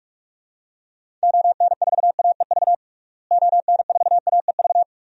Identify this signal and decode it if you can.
Morse code, heard as a keyed tone, about 46 wpm, 700 Hz. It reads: ON4AEV ON4AEV
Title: DXCC-100 Call Signs - 6 Characters - Repeated 3x - 45wpm